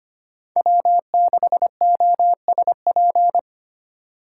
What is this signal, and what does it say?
Morse code, heard as a keyed tone, about 25 wpm, 700 Hz. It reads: W6OSP
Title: K1USN SST QSO Practice - 25wpm